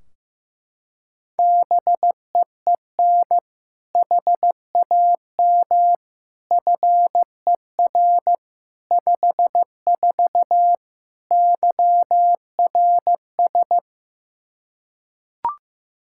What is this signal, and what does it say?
Morse code, heard as a keyed tone, about 15 wpm, 700 Hz. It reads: BEEN HAM FER 54 YRS
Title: K1USN SST QSO Practice 15wpm